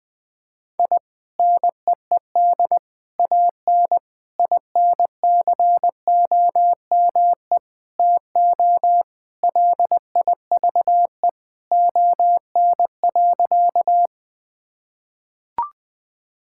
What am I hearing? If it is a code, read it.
Morse code, heard as a keyed tone, about 20 wpm, 700 Hz. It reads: I NEED AN INCOME TO LIVE ON.
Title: Sentences from Top 1000 Words - Review Part 5 - 20wpm